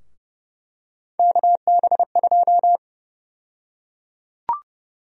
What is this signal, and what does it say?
Morse code, heard as a keyed tone, about 32 wpm, 700 Hz. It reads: KB2
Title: USA Call Sign Prefixes - Repeated 2x - 30wpm